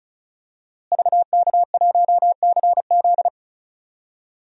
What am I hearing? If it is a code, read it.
Morse code, heard as a keyed tone, about 36 wpm, 700 Hz. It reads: VK1CZ
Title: DXCC-100 Call Signs - 5 Characters - Repeated 3x - 35wpm